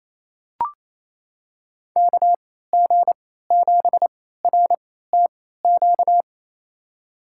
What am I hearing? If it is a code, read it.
Morse code, heard as a keyed tone, about 28 wpm, spaced as for 16 wpm, 700 Hz. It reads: KG7RTQ
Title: USA Call Signs - 6 Characters - 3x Character Spacing - 28wpm